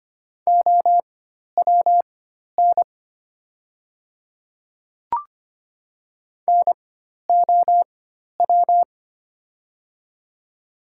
Morse code, heard as a keyed tone, about 25 wpm, spaced as for 12 wpm, 700 Hz. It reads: OWN NOW
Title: 4x Character Spacing - Top 300 Words - 25wpm